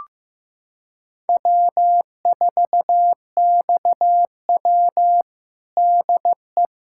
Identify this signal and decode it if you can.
Morse code, heard as a keyed tone, about 15 wpm, 700 Hz. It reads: W4XW DE
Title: USA Callsign DE Callsign - 4 Characters - 15wpm